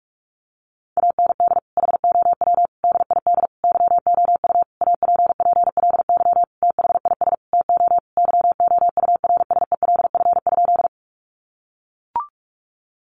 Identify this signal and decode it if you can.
Morse code, heard as a keyed tone, about 44 wpm, 700 Hz. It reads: AND HOW DID YOU APPLY THIS TO YOURSELF?
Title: Sentences from Top 1200 Words 45wpm